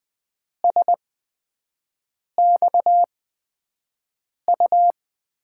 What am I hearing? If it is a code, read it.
Morse code, heard as a keyed tone, about 20 wpm, spaced as for 5 wpm, 700 Hz. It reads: SXU
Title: USA Call Signs - 6 Characters - 8x Character Spacing - 20wpm